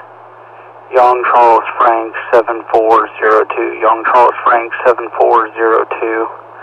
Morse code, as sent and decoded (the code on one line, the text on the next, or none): none